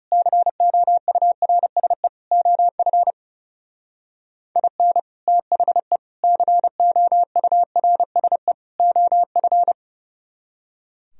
Morse code, heard as a keyed tone, about 36 wpm, 700 Hz. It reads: COURSE OF IN THE COURSE OF
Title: Speed-Racing - Sets of 4 Words 30wpm